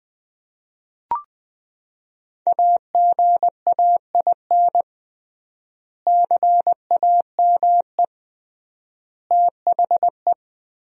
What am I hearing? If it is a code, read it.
Morse code, heard as a keyed tone, about 20 wpm, 700 Hz. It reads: AGAIN CAME THE
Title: 3x Word-Spacing - Sentences from Top 500 Words - 20wpm